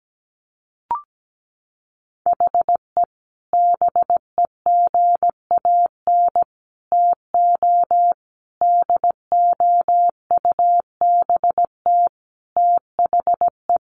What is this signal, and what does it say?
Morse code, heard as a keyed tone, about 17 wpm, 700 Hz. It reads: HE BEGAN TO DOUBT THE
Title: Sentences from Top 1500 Words 17wpm